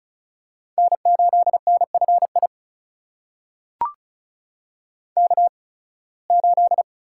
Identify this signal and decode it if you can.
Morse code, heard as a keyed tone, about 36 wpm, spaced as for 23 wpm, 700 Hz. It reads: N8NFI K8
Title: USA Call Signs - 5 Characters - 8x Character Spacing - 35wpm